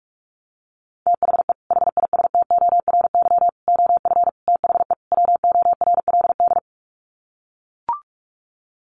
Morse code, heard as a keyed tone, about 44 wpm, 700 Hz. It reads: THE HISTORY OF THE WORLD
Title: Sets of 5 Words Encore2 45wpm